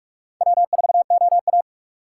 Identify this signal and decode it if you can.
Morse code, heard as a keyed tone, about 46 wpm, 700 Hz. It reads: W4OA
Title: USA Call Signs - 4 Characters - 6x Character Spacing - 45wpm